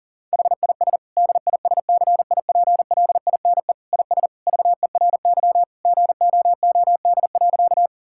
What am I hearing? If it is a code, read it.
Morse code, heard as a keyed tone, about 40 wpm, 700 Hz. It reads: HIS DISCIPLINE IS VERY GOOD.
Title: Sentences from Top 1000 Words - Review Part 6 - 40wpm